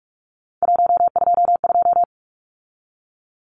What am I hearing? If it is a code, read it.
Morse code, heard as a keyed tone, about 44 wpm, 700 Hz. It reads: 122
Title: Numbers - 3-Digit 45wpm